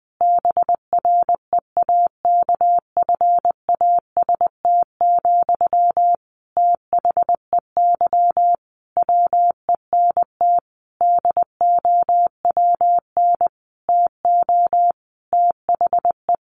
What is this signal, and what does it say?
Morse code, heard as a keyed tone, about 20 wpm, 700 Hz. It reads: BREAKFAST, THEY WENT DOWN TO THE
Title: Sentences from Top 1500 Words 20wpm